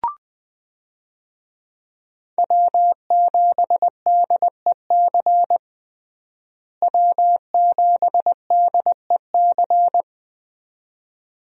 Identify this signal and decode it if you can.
Morse code, heard as a keyed tone, about 20 wpm, 700 Hz. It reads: W7DEC W7DEC
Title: USA Call Signs - 5 Characters - Repeated 2x - 20wpm